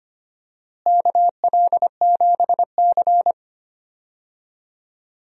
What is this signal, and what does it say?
Morse code, heard as a keyed tone, about 25 wpm, 700 Hz. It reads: KL7C
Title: DXCC-100 Call Signs - 4 Characters - Repeated 2x - 25wpm